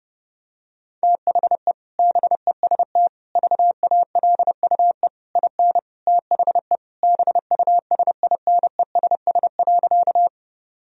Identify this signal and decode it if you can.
Morse code, heard as a keyed tone, about 30 wpm, 700 Hz. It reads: THE BEST VALUE IN THE BUSINESS.
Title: Sentences from Top 1000 Words - Review Part 3 - 30wpm